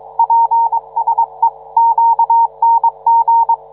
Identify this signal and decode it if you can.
Morse code, heard as a keyed tone, about 22 wpm, 910 Hz. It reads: PSEQNG